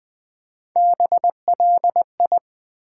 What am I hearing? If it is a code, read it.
Morse code, heard as a keyed tone, about 20 wpm, 700 Hz. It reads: BLI
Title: Build A Word - 1 Letter At A Time - 3 to 5-Letter Words - 20wpm